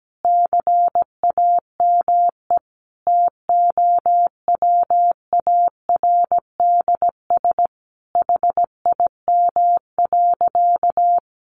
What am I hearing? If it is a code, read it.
Morse code, heard as a keyed tone, about 17 wpm, 700 Hz. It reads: CAME TOWARDS HIM.
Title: Sentences from Top 1100 Words 17wpm